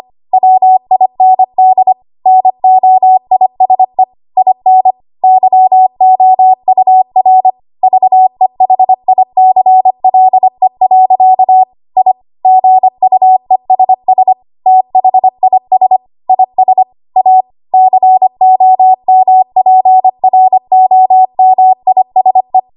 Morse code, heard as a keyed tone, about 25 wpm, 750 Hz. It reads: WIND NOISE IN YOUR VEHICLE. I GUESS THIS IS A COMPROMISE
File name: Archive / 25 WPM / 25WPM